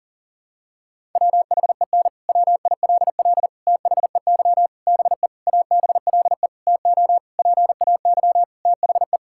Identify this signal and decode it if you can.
Morse code, heard as a keyed tone, about 40 wpm, 700 Hz. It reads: WHEN WILL THEY BE ABLE TO PAY THE
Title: Sentences from Top 1000 Words - Review Part 5 - 40wpm